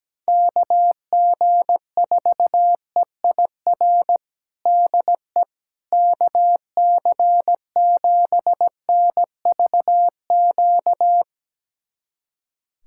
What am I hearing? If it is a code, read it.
Morse code, heard as a keyed tone, about 17 wpm, 700 Hz. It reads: KG4EIR DE KC7NVQ
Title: USA Callsign DE Callsign 17wpm